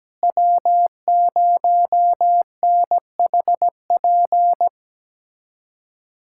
Morse code, heard as a keyed tone, about 17 wpm, 700 Hz. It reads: W0NHP
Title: USA Callsign DE Callsign - 5 Characters - 17wpm